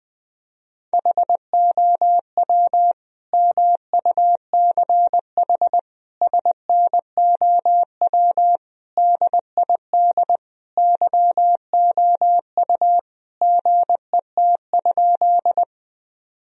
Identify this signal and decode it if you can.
Morse code, heard as a keyed tone, about 20 wpm, 700 Hz. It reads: HOW MUCH SNOW DID YOU GET?
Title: Sentences from Top 1000 Words - Review Part 3 - 20wpm